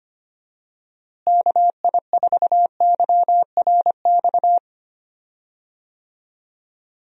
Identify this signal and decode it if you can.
Morse code, heard as a keyed tone, about 25 wpm, 700 Hz. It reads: KI4YRX
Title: USA Call Signs - 6 Characters - Repeated 2x - 25wpm